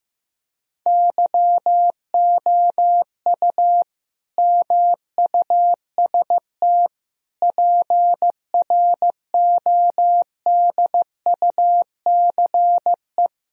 Morse code, heard as a keyed tone, about 15 wpm, 700 Hz. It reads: YOU MUST PRODUCE
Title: Sentences from Top 1000 Words - Review Part 3 - 15wpm